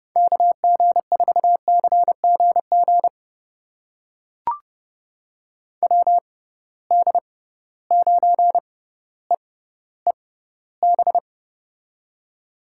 Morse code, heard as a keyed tone, about 30 wpm, spaced as for 15 wpm, 700 Hz. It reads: KG4CGG WD9EEB
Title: USA Call Signs - 6 Characters - 6x Character Spacing - 30wpm